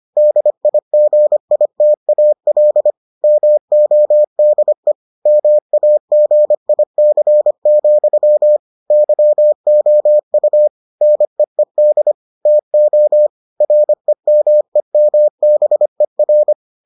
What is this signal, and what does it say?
Morse code, heard as a keyed tone, about 25 wpm, 600 Hz. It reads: DIGITAL MODE MAGIC, YOU NEED TO REMEMBER